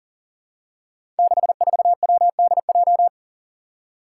Morse code, heard as a keyed tone, about 40 wpm, 700 Hz. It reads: 64WDJ
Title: Five Challenging Letter-Number Combinations 40wpm